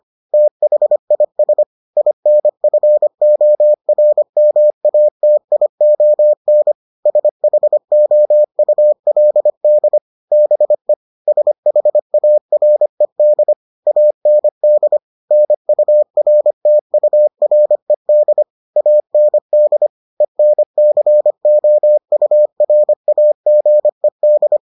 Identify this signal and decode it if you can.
Morse code, heard as a keyed tone, about 25 wpm, 600 Hz. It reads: THIS INFORMATION SHOULD BE SHARED AND NURTURED AND ENCOURAGED